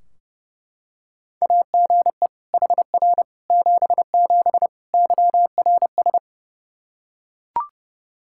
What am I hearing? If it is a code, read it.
Morse code, heard as a keyed tone, about 32 wpm, 700 Hz. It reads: AGE HR 77 YRS
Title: Practice QSO - Exchange 3 - More Info - 30wpm